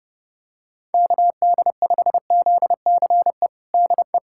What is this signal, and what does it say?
Morse code, heard as a keyed tone, about 30 wpm, 700 Hz. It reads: KD5ZCE DE
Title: USA Callsign DE Callsign - 6 Characters - 30wpm